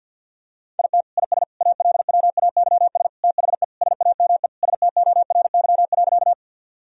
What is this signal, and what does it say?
Morse code, heard as a keyed tone, about 52 wpm, 700 Hz. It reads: IT IS ALWAYS THE SAME STORY.
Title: Sentences from Top 1000 Words - Review Part 1 - 50wpm